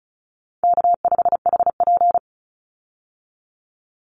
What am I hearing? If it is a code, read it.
Morse code, heard as a keyed tone, about 34 wpm, 700 Hz. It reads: K5HP
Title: Call Signs Repeated 2x - 35wpm